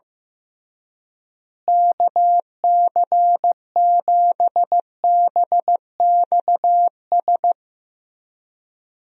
Morse code, heard as a keyed tone, about 15 wpm, 700 Hz. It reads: KC7BXS
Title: USA Call Signs - 6 Characters - Repeated 3x - 15wpm